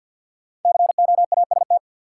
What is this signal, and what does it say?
Morse code, heard as a keyed tone, about 53 wpm, 700 Hz. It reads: COAST
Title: Sentences from Top 1300 Words - New Word Drill Alt - 50wpm